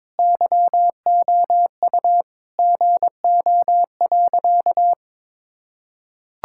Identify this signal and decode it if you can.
Morse code, heard as a keyed tone, about 22 wpm, 700 Hz. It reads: YOU GO.
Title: Sentences from Top 1000 Words - Review Part 6 - 22wpm